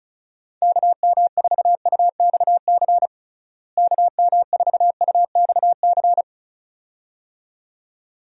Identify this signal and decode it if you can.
Morse code, heard as a keyed tone, about 34 wpm, 700 Hz. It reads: KM4UXC KM4UXC
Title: USA Call Signs - 6 Characters - Repeated 2x - 35wpm